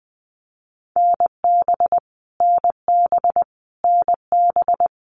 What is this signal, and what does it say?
Morse code, heard as a keyed tone, about 20 wpm, 700 Hz. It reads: NB NB NB
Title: Canadian Provinces and Territories - Repeated 3x - 20wpm